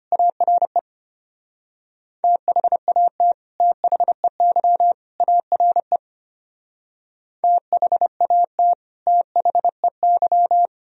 Morse code, heard as a keyed tone, about 31 wpm, 700 Hz. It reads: ARE THAT THEY ARE THAT THEY
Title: Speed-Racing - Sets of 3 Words 25wpm